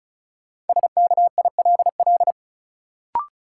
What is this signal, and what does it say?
Morse code, heard as a keyed tone, about 36 wpm, 700 Hz. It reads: SKILL
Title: Sentences from Top 1200 Words - New Word Drill - 35wpm